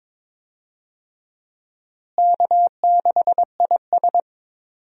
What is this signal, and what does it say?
Morse code, heard as a keyed tone, about 22 wpm, 700 Hz. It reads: K6IS
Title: USA Call Signs - 4 Characters - Repeated 3x - 22wpm